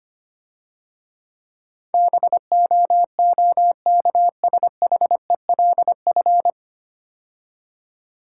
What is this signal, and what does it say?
Morse code, heard as a keyed tone, about 25 wpm, 700 Hz. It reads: BOOKSHELF
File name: Easy Words - Speed-Racing with Hints - 25wpm